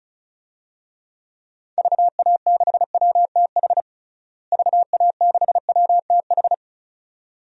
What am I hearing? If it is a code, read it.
Morse code, heard as a keyed tone, about 36 wpm, 700 Hz. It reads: VA6WTH VA6WTH
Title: DXCC-100 Call Signs - 6 Characters - Repeated 2x - 35wpm